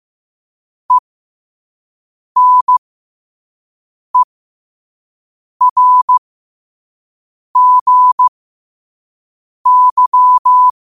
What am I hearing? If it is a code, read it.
Morse code, heard as a keyed tone, about 15 wpm, spaced as for 5 wpm, 1000 Hz. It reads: ENERGY
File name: qotdfiles / 5wpm